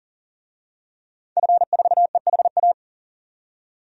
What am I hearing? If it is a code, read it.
Morse code, heard as a keyed tone, about 40 wpm, 700 Hz. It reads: F4EHA
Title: DXCC-100 Call Signs - 5 Characters - 40wpm